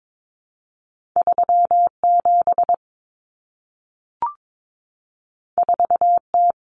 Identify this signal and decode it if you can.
Morse code, heard as a keyed tone, about 22 wpm, 700 Hz. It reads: 37 4T
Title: Common Cut Numbers - 2-Digit 22wpm